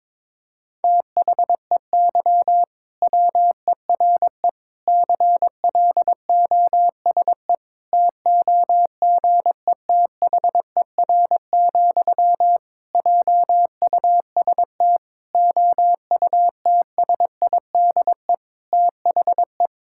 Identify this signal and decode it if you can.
Morse code, heard as a keyed tone, about 22 wpm, 700 Hz. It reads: THEY WERE CLOSE TOGETHER, JUST OUTSIDE THE
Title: Sentences from Top 1000 Words - Review Part 6 - 22wpm